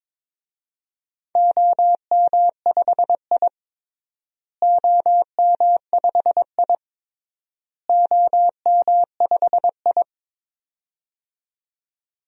Morse code, heard as a keyed tone, about 21 wpm, 700 Hz. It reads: OM5I OM5I OM5I
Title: DXCC-100 Call Signs - 4 Characters - Repeated 3x - 22wpm